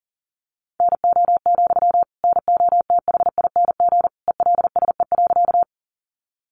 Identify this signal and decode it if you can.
Morse code, heard as a keyed tone, about 40 wpm, 700 Hz. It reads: NO, NOTHING ELSE.
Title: Sentences from Top 800 Words 40wpm